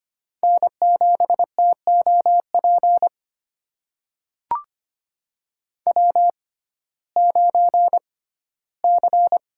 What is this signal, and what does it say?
Morse code, heard as a keyed tone, about 25 wpm, spaced as for 15 wpm, 700 Hz. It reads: N7TOP W9C
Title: USA Call Signs - 5 Characters - 6x Character Spacing - 25wpm